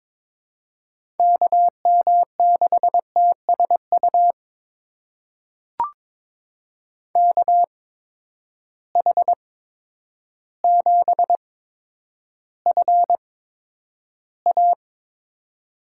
Morse code, heard as a keyed tone, about 22 wpm, spaced as for 10 wpm, 700 Hz. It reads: KM6TSU KH7FA
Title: USA Call Signs - 8x Character Spacing - 22wpm